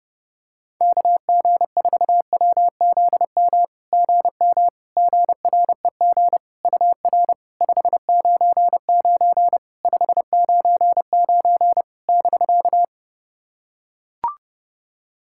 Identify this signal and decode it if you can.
Morse code, heard as a keyed tone, about 30 wpm, 700 Hz. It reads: KG4WZM GM GREG UR 599 599 <BK>
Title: SOTA QSO Practice 30wpm